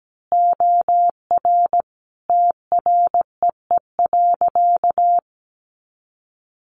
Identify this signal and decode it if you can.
Morse code, heard as a keyed tone, about 17 wpm, 700 Hz. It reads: OR TREE.
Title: A Sound Of Thunder 17wpm